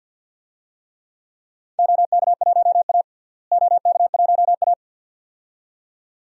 Morse code, heard as a keyed tone, about 51 wpm, 700 Hz. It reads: OK1A OK1A
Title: DXCC-100 Call Signs - 4 Characters - Repeated 2x - 50wpm